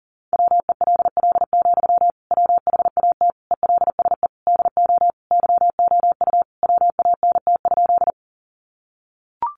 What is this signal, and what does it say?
Morse code, heard as a keyed tone, about 40 wpm, 700 Hz. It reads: WELL, WHAT ELSE DO YOU WANT?
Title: Sentences from Top 800 Words 40wpm